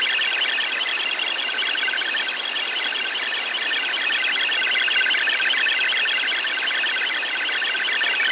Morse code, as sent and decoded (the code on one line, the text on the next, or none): none